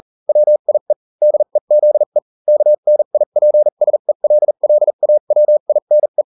Morse code, heard as a keyed tone, about 40 wpm, spaced as for 35 wpm, 600 Hz. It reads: WIE DEZE KNIPSELLAWINE